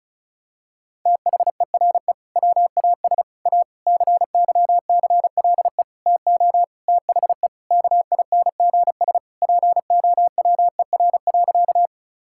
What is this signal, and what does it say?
Morse code, heard as a keyed tone, about 36 wpm, 700 Hz. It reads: THERE WAS A CYCLE TO THE KINGS POWER.
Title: Sentences from Top 1000 Words - Review Part 6 - 35wpm